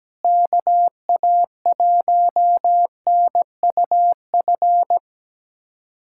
Morse code, heard as a keyed tone, about 17 wpm, 700 Hz. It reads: KA1NUF
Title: USA Callsign DE Callsign - 6 Characters - 17wpm